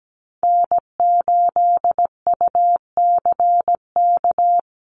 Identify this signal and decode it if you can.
Morse code, heard as a keyed tone, about 17 wpm, 700 Hz. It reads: N8UCK
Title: USA Call Signs Repeated 2x 17wpm